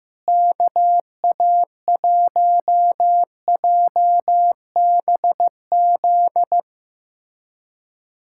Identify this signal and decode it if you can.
Morse code, heard as a keyed tone, about 16 wpm, 700 Hz. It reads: KA1JBZ
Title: SOTA QSO Practice 15wpm